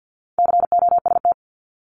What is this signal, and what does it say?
Morse code, heard as a keyed tone, about 50 wpm, 700 Hz. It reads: COST
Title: Multi-letter Phonemes 50wpm